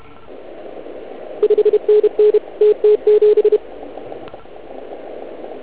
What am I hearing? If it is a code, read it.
Morse code, heard as a keyed tone, about 32 wpm, 420 Hz. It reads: 5NN TT7